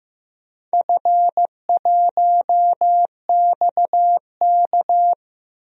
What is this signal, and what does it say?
Morse code, heard as a keyed tone, about 15 wpm, 700 Hz. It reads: F1XK
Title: DXCC-100 Call Signs - 4 Characters - Repeated 2x - 15wpm